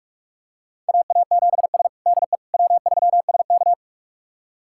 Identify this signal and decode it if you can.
Morse code, heard as a keyed tone, about 47 wpm, 690 Hz. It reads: AA7S DE W3SK
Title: USA Callsign DE Callsign - 4 Characters - 45wpm